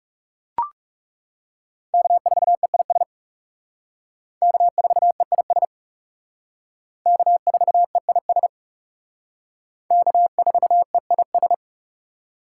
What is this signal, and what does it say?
Morse code, heard as a keyed tone, about 37 wpm, 700 Hz. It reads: K4EIS K4EIS K4EIS K4EIS
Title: Speed Racing - USA 1x3 Call Signs 30wpm